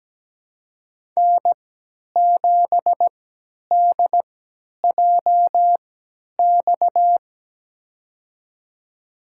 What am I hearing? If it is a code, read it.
Morse code, heard as a keyed tone, about 17 wpm, spaced as for 10 wpm, 700 Hz. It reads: N7DJX
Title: USA Call Signs - 5 Characters - 3x Character Spacing - 17wpm